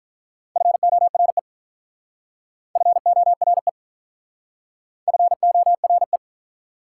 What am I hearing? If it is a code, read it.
Morse code, heard as a keyed tone, about 44 wpm, 700 Hz. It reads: FORE FORE FORE